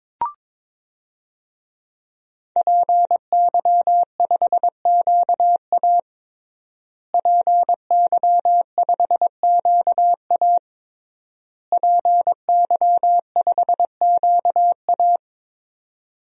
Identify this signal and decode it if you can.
Morse code, heard as a keyed tone, about 22 wpm, 700 Hz. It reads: PY5QA PY5QA PY5QA
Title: DXCC-100 Call Signs - 5 Characters - Repeated 3x - 22wpm